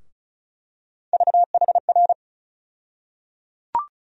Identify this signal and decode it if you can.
Morse code, heard as a keyed tone, about 34 wpm, 700 Hz. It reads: VHR
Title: USA Call Sign Postfixes - 3 Letters 35wpm